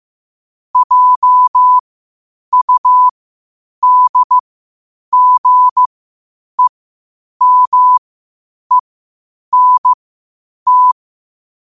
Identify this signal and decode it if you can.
Morse code, heard as a keyed tone, about 15 wpm, spaced as for 8 wpm, 1000 Hz. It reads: JUDGEMENT
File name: qotdfiles / 8wpm